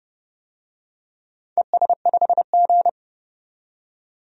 Morse code, heard as a keyed tone, about 30 wpm, 700 Hz. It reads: ES5G
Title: DXCC-100 Call Signs - 4 Characters - 30wpm